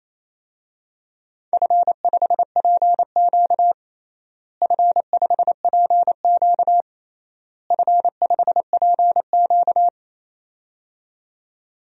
Morse code, heard as a keyed tone, about 28 wpm, 700 Hz. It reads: F5PQ F5PQ F5PQ
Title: DXCC-100 Call Signs - 4 Characters - Repeated 3x - 28wpm